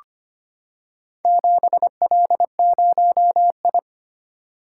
Morse code, heard as a keyed tone, about 25 wpm, 700 Hz. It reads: 7L0I
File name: practice / 25wpm / Four Letter-Number Combinations - 25wpm